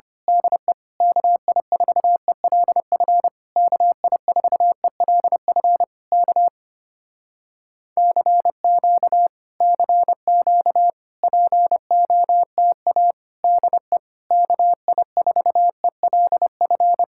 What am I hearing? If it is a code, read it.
Morse code, heard as a keyed tone, about 24 wpm, 700 Hz. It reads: DE KI4ELF KI4ELF K CQ CQ POTA DE KI4ELF
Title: Speed-Racing Common QSO Phrases Encore - 25wpm